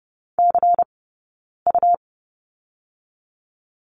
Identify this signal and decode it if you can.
Morse code, heard as a keyed tone, about 30 wpm, 700 Hz. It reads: C U
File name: Lesson 19d - Letter w - Two Characters 30wpm